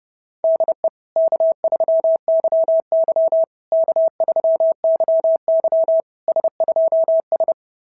Morse code, heard as a keyed tone, about 27 wpm, 650 Hz. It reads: DE K3YY K3YY S2S
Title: SOTA QSO Practice 30wpm